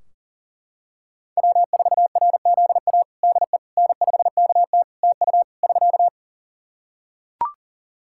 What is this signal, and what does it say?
Morse code, heard as a keyed tone, about 40 wpm, 700 Hz. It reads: W4RZA DE N5KT TU <SK>
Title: Practice QSO - Exchange 4 - Ending - 40wpm